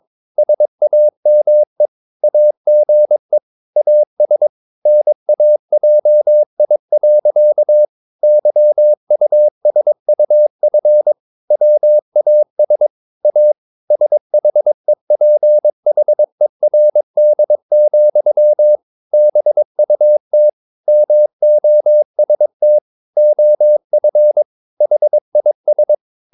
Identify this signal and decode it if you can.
Morse code, heard as a keyed tone, about 22 wpm, 600 Hz. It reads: SAME AGE AS NAJI. YUSUF WAS A SHEPHERD, BUT MOST OF HIS
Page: Romanen Never af Ken Follet på engelsk.